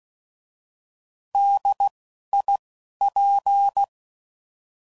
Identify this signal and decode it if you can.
Morse code, heard as a keyed tone, about 16 wpm, 800 Hz. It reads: D I P
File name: Prefixes / By First Letter / Ds